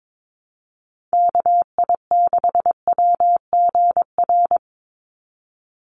Morse code, heard as a keyed tone, about 22 wpm, 700 Hz. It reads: KI6WGR
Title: USA Call Signs 22wpm